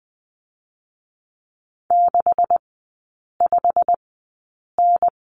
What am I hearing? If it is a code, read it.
Morse code, heard as a keyed tone, about 20 wpm, spaced as for 8 wpm, 700 Hz. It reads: 65N
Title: 127 Lesson 21e - Number 6 - Three Characters 20wpm